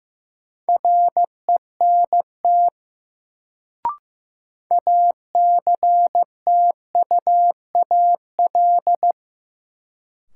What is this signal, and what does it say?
Morse code, heard as a keyed tone, about 15 wpm, 700 Hz. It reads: RENT ACTUAL
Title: Sentences from Top 1000 Words - Review Part 5 - New Word Drill - 15wpm